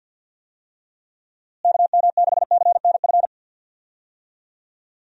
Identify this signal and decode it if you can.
Morse code, heard as a keyed tone, about 53 wpm, 700 Hz. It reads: KM6CNF
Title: USA Call Signs - 6 Characters - Repeated 2x - 50wpm